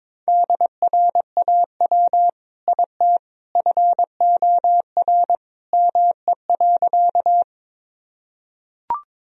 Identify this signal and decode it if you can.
Morse code, heard as a keyed tone, about 22 wpm, 700 Hz. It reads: DRAW IT FOR ME.
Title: Sentences from Top 1000 Words - Review Part 1 - 22wpm